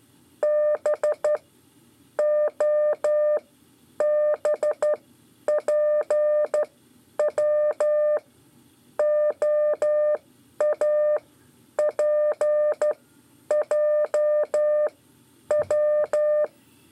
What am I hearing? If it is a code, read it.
Morse code, heard as a keyed tone, about 12 wpm, spaced as for 8 wpm, 600 Hz. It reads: BOBPWOAPJW
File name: 5 LEZIONE RIPASSO 1